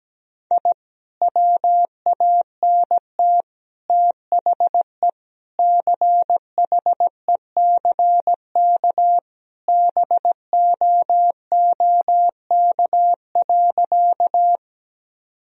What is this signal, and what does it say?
Morse code, heard as a keyed tone, about 17 wpm, 700 Hz. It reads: I WANT THE CHECK BOOK.
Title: Sentences from Top 1000 Words - Review Part 3 - 17wpm